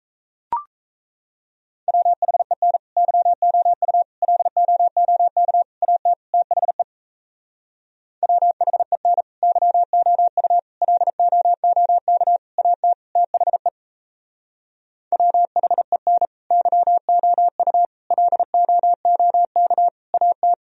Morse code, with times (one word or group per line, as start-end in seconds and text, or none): 1.9-2.8 WHEN
3.0-4.0 YOU
4.2-5.6 LOOK
5.8-6.1 AT
6.3-6.8 THE
8.2-9.2 WHEN
9.4-10.6 YOU
10.8-12.4 LOOK
12.6-12.9 AT
13.2-13.7 THE
15.1-16.3 WHEN
16.5-17.9 YOU
18.1-19.9 LOOK
20.1-20.5 AT